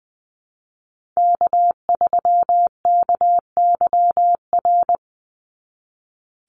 Morse code, heard as a keyed tone, about 20 wpm, 700 Hz. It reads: K3KYR
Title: NA SOTA Chaser Callsigns 20wpm